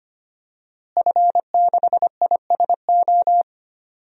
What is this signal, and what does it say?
Morse code, heard as a keyed tone, about 25 wpm, 700 Hz. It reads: F6ISO
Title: DXCC-100 Call Signs - 5 Characters - Repeated 3x - 25wpm